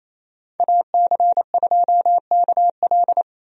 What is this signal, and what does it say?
Morse code, heard as a keyed tone, about 28 wpm, 700 Hz. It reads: AC2KL
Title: USA Call Signs - 5 Characters - 28wpm